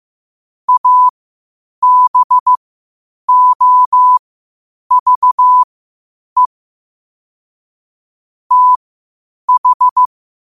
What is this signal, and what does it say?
Morse code, heard as a keyed tone, about 15 wpm, spaced as for 8 wpm, 1000 Hz. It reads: ABOVE TH
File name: qotdfiles / 8wpm